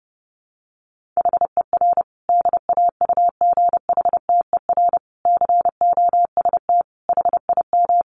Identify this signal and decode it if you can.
Morse code, heard as a keyed tone, about 30 wpm, 700 Hz. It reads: HER DAUGHTER COST HIM
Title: Sentences from Top 1500 Words 30wpm